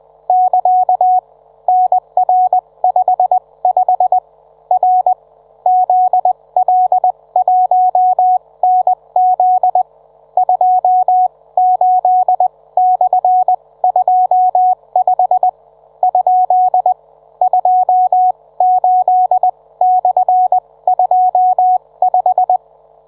Morse code, heard as a keyed tone, about 20 wpm, spaced as for 17 wpm, 740 Hz. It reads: <KA> NR55 R ZL1NZ 28/25 ? 28/25